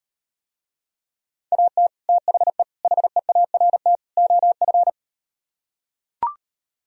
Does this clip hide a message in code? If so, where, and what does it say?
Yes: Morse code from 1.5 to 4.9 s, heard as a keyed tone, about 38 wpm, 700 Hz. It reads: AT THE HEART OF